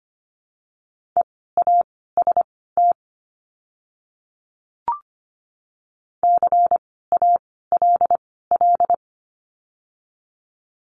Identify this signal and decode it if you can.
Morse code, heard as a keyed tone, about 25 wpm, spaced as for 16 wpm, 700 Hz. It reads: EAST CALL
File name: course / 25:10wpm / 094 Lesson 16f - Letter c - Words 25wpm